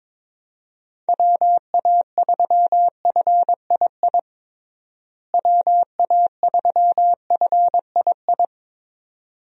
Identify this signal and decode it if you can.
Morse code, heard as a keyed tone, about 22 wpm, 700 Hz. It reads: WA3FII WA3FII
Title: USA Call Signs - 6 Characters - Repeated 2x - 22wpm